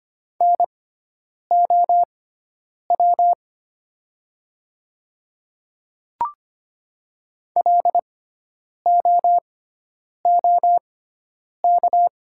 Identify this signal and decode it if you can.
Morse code, heard as a keyed tone, about 25 wpm, spaced as for 9 wpm, 700 Hz. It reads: NOW LOOK
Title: 6x Character Spacing - Top 200 Words - 25wpm